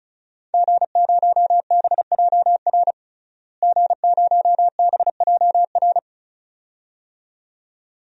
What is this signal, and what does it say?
Morse code, heard as a keyed tone, about 34 wpm, 700 Hz. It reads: G0BJR G0BJR
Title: DXCC-100 Call Signs - 5 Characters - Repeated 2x - 35wpm